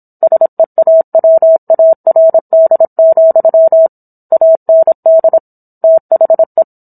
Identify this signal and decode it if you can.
Morse code, heard as a keyed tone, about 27 wpm, 650 Hz. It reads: SEAWARD, AND THE